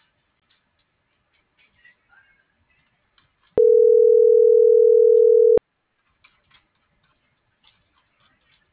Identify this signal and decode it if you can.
Ringback tone (North America), 440 + 480 Hz.